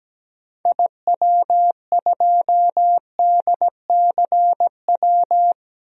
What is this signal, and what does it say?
Morse code, heard as a keyed tone, about 17 wpm, 700 Hz. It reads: IW2DCW
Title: DXCC-100 Call Signs - 6 Characters - 17wpm